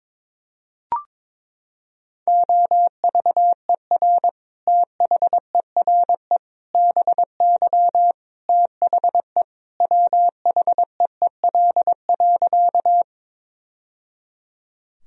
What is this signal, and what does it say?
Morse code, heard as a keyed tone, about 22 wpm, 700 Hz. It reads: OVER THERE BY THE WHEEL.
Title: Sentences from Top 1000 Words - Review Part 3 - 22wpm